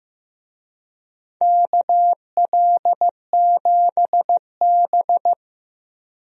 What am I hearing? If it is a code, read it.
Morse code, heard as a keyed tone, about 15 wpm, 700 Hz. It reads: KL7B
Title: DXCC-100 Call Signs - 4 Characters - Repeated 3x - 15wpm